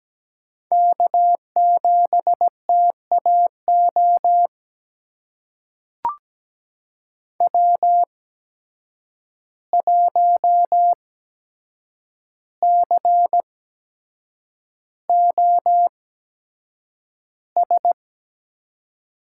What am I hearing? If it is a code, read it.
Morse code, heard as a keyed tone, about 17 wpm, spaced as for 7 wpm, 700 Hz. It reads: K7TAO W1COS
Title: USA Call Signs - 5 Characters - 8x Character Spacing - 17wpm